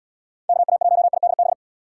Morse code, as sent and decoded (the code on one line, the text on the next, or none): -.- .- .---- .. -.. -...
KA1IDB